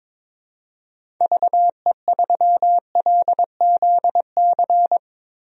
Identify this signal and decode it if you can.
Morse code, heard as a keyed tone, about 22 wpm, 700 Hz. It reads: VE3LZC
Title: DXCC-100 Call Signs - 6 Characters - 22wpm